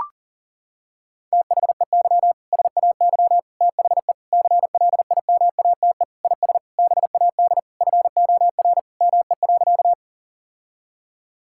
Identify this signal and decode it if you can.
Morse code, heard as a keyed tone, about 40 wpm, 700 Hz. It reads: THEY SAY THE CLIMATE IS BAD FOR ME.
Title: Sentences from Top 1600 Words 40wpm